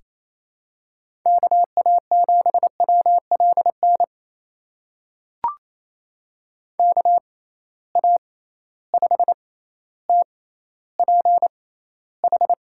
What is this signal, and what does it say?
Morse code, heard as a keyed tone, about 28 wpm, spaced as for 14 wpm, 700 Hz. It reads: KA7WLN KA5TPH
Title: USA Call Signs - 6 Characters - 6x Character Spacing - 28wpm